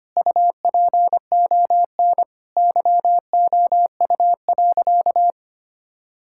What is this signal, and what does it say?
Morse code, heard as a keyed tone, about 25 wpm, 700 Hz. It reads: UPON YOU.
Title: Sentences from Top 1600 Words 25wpm